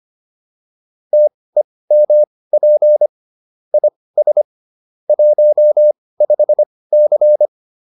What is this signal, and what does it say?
Morse code, heard as a keyed tone, about 25 wpm, spaced as for 18 wpm, 600 Hz. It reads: TEMP IS 15C